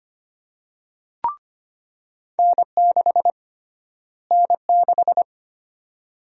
Morse code, heard as a keyed tone, about 25 wpm, 700 Hz. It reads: N6 N6
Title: USA Call Sign Prefixes - Repeated 2x - 25wpm